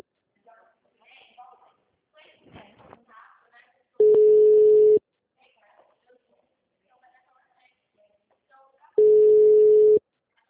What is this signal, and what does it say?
Ringback tone (Europe, 425 Hz), with speech.